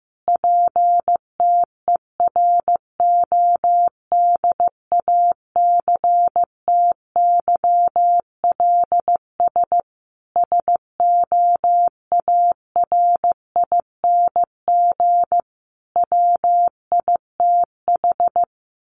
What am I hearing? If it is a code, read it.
Morse code, heard as a keyed tone, about 14 wpm, 700 Hz. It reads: PTERODACTYLS SOARING WITH
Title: A Sound Of Thunder 15wpm